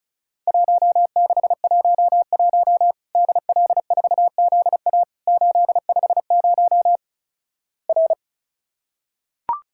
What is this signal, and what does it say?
Morse code, heard as a keyed tone, about 36 wpm, 700 Hz. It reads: 1611 DL4ZA 850 R
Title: Worked All Europe DX Contest - QTC Exchange - 35wpm